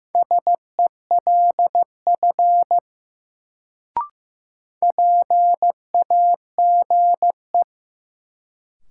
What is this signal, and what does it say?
Morse code, heard as a keyed tone, about 15 wpm, 700 Hz. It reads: SELF PAGE
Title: Sentences from Top 1000 Words - Review Part 1 - New Word Drill - 15wpm